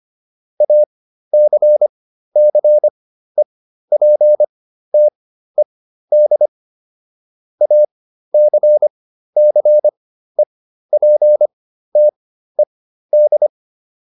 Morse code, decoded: ACCEPTED ACCEPTED

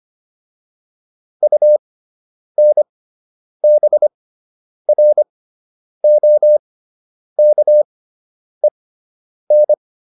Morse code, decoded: UNBROKEN

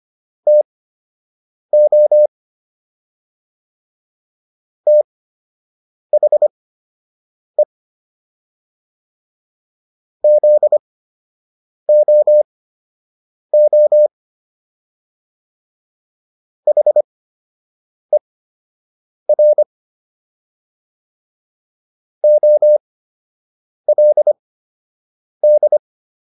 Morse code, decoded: TO THE ZOO HER OLD

